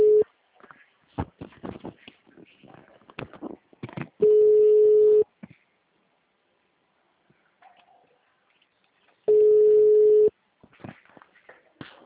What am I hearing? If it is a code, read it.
Ringback tone (Europe, 425 Hz).